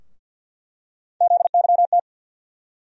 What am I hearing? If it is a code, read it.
Morse code, heard as a keyed tone, about 50 wpm, 700 Hz. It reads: ZYT
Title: USA Call Sign Postfixes - 3 Letters 50wpm